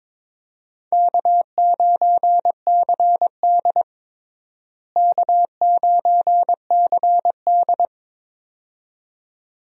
Morse code, heard as a keyed tone, about 21 wpm, 700 Hz. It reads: K9CD K9CD
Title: USA Call Signs - 4 Characters - Repeated 2x - 22wpm